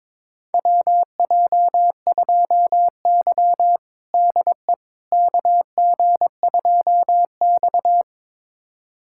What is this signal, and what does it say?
Morse code, heard as a keyed tone, about 22 wpm, 700 Hz. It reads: WJ2Y DE KG2X
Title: USA Callsign DE Callsign - 4 Characters - 22wpm